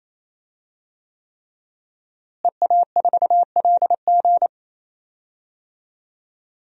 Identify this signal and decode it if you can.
Morse code, heard as a keyed tone, about 28 wpm, 700 Hz. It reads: EA4LG
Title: DXCC-100 Call Signs - 5 Characters - Repeated 3x - 28wpm